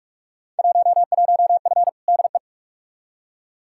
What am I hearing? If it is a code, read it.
Morse code, heard as a keyed tone, about 47 wpm, 700 Hz. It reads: 11F DE
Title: Field Day Exchanges Encore - 45wpm